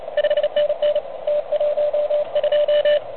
Morse code, heard as a keyed tone, about 30 wpm, 590 Hz, with radio noise. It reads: 5NN T12
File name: truSDX / MWC29.8